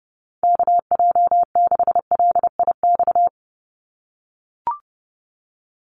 Morse code, heard as a keyed tone, about 30 wpm, 700 Hz. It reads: KJ6LIX
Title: USA Call Signs 30wpm